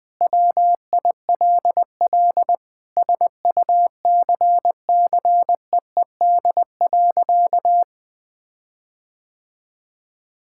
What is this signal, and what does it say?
Morse code, heard as a keyed tone, about 19 wpm, 700 Hz. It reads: WILL SUCCEED.
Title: Sentences with 8-Word Phrases - Spoken Beginning - 20wpm